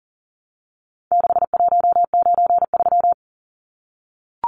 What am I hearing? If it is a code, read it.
Morse code, heard as a keyed tone, about 40 wpm, 700 Hz. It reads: 6193
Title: Numbers - 4-Digit 40wpm